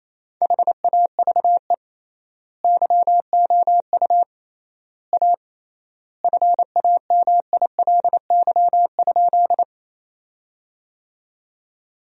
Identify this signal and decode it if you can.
Morse code, heard as a keyed tone, about 28 wpm, 700 Hz. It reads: HAVE YOU A FAMILY?
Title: 3x Word-Spacing - Sentences from Top 400 Words - 28wpm